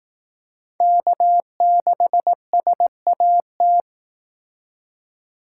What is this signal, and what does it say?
Morse code, heard as a keyed tone, about 18 wpm, 700 Hz. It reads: K6SAT
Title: Speed Racing - USA 1x3 Call Signs 15wpm